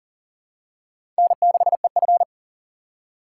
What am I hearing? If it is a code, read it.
Morse code, heard as a keyed tone, about 40 wpm, 700 Hz. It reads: N6EF
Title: USA Call Signs - 4 Characters - 4x Character Spacing - 40wpm